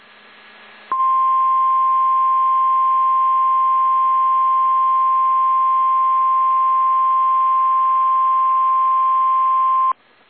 Emergency Alert System (SAME) tones (the 1050 Hz attention tone). It is heard from 0.9 to 10.0 s.
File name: eas / KSDZKDJL / KSDZKDJL